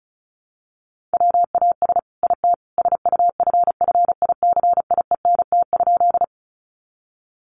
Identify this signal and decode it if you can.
Morse code, heard as a keyed tone, about 34 wpm, 700 Hz. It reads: WAS IT SUFFICIENT?
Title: Sentences from Top 1400 Words 35wpm